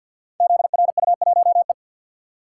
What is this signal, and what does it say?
Morse code, heard as a keyed tone, about 48 wpm, 700 Hz. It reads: ZRU1E
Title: Code Groups - 5 Letter:Numbers - 50wpm